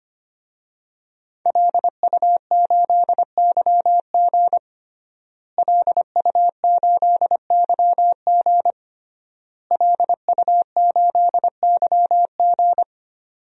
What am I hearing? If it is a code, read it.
Morse code, heard as a keyed tone, about 25 wpm, 700 Hz. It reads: LU8YG LU8YG LU8YG
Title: DXCC-100 Call Signs - 5 Characters - Repeated 3x - 25wpm